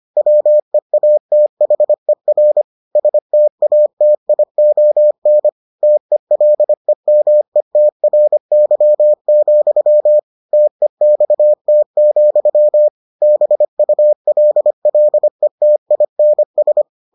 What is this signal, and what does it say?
Morse code, heard as a keyed tone, about 25 wpm, 600 Hz. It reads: WEATHER STATION TELEMETRY, TEXT, BULLETINS